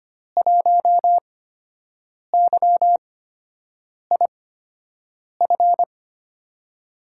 Morse code, heard as a keyed tone, about 25 wpm, spaced as for 7 wpm, 700 Hz. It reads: 1YIF
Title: USA Call Signs - 5 Characters - 8x Character Spacing - 25wpm